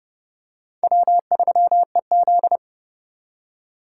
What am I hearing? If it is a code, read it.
Morse code, heard as a keyed tone, about 30 wpm, 700 Hz. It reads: W3EZ